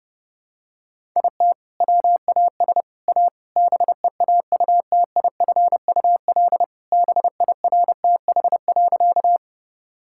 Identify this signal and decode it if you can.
Morse code, heard as a keyed tone, about 30 wpm, 700 Hz. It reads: IT WAS A BEAUTIFUL BIRTH.
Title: Sentences from Top 1000 Words - Review Part 5 - 30wpm